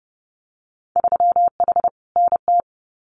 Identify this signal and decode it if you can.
Morse code, heard as a keyed tone, about 30 wpm, 700 Hz. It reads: 3H NT
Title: Winter Field Day Exchanges Encore 30wpm